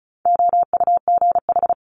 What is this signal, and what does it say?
Morse code, heard as a keyed tone, about 36 wpm, 700 Hz. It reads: OUGH
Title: Multi-letter Phonemes Quick Fire 35wpm